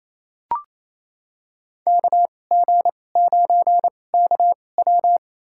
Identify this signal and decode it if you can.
Morse code, heard as a keyed tone, about 28 wpm, spaced as for 20 wpm, 700 Hz. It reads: KG9KW
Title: USA Call Signs - 5 Characters - 2x Character Spacing - 28wpm